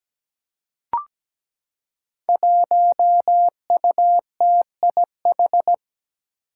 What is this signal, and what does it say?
Morse code, heard as a keyed tone, about 17 wpm, 700 Hz. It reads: 1UTIH
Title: Five Challenging Letter-Number Combinations 17wpm